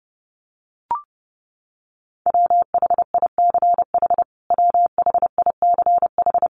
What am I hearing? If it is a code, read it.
Morse code, heard as a keyed tone, about 30 wpm, 700 Hz. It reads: WHICH WHICH
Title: Top 500 Words Repeated 30wpm